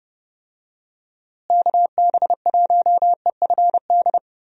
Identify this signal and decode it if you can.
Morse code, heard as a keyed tone, about 32 wpm, 700 Hz. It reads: KB1EFD
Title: USA Call Signs - 6 Characters - Repeated 2x - 30wpm